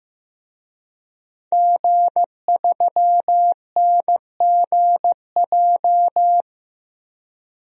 Morse code, heard as a keyed tone, about 15 wpm, 700 Hz. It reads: G3NGJ
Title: DXCC-100 Call Signs - 5 Characters - 15wpm